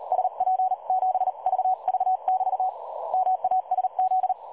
Morse code, heard as a keyed tone, about 42 wpm, spaced as for 36 wpm, 760 Hz. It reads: SP7VV <BT> MASZ
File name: dlazaawansowanych / nagraniazpasma / websdr recording 3565.0kHz